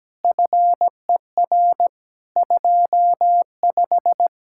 Morse code, heard as a keyed tone, about 17 wpm, 700 Hz. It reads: FER 25
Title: Practice QSO - Exchange 3 - More Info - 17wpm